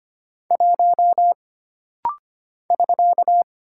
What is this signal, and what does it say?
Morse code, heard as a keyed tone, about 25 wpm, 700 Hz. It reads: 1 <SK>
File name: Single Character (Letter:Number:Punctuation:Prosign) - Rapid-Fire - 25wpm